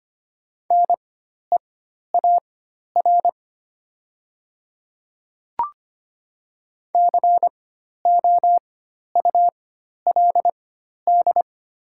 Morse code, heard as a keyed tone, about 25 wpm, spaced as for 12 wpm, 700 Hz. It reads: NEAR COULD
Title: 4x Character Spacing - Top 200 Words - 25wpm